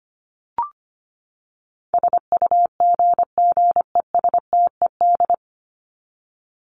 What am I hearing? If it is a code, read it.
Morse code, heard as a keyed tone, about 25 wpm, 700 Hz. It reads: SUGGESTED
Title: ED Suffix - Top 100 Words with Base Words 25wpm